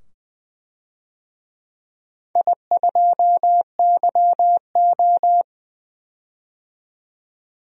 Morse code, heard as a keyed tone, about 20 wpm, 700 Hz. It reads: I2YO
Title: DXCC-100 Call Signs - 4 Characters - 20wpm